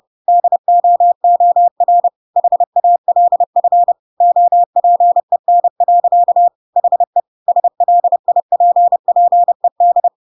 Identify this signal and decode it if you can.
Morse code, heard as a keyed tone, about 30 wpm, 700 Hz. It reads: DOOR HALF OPEN. HE SLIPPED